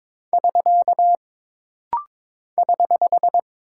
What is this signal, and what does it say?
Morse code, heard as a keyed tone, about 22 wpm, 700 Hz. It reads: <SK> <HH>
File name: practice / 22wpm / Single Punctuation-Prosign - Rapid-Fire - 22wpm